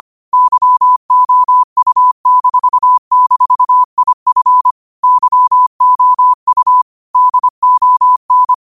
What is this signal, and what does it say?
Morse code, heard as a keyed tone, about 25 wpm, 1000 Hz. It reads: YOU--IF YOU DON